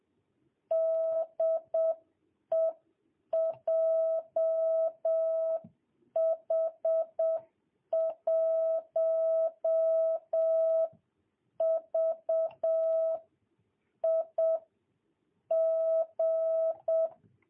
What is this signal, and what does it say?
Morse code, heard as a keyed tone, about 7 wpm, spaced as for 6 wpm, 650 Hz. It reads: DE JH1VIG
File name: jh8cbh / 2022cw